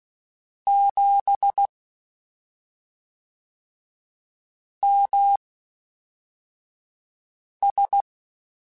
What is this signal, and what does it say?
Morse code, heard as a keyed tone, about 16 wpm, spaced as for 3 wpm, 800 Hz. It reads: 7MS
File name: CW Comparisons / CW Comparisons-mp3s / CW Comparisons Broken Characters @5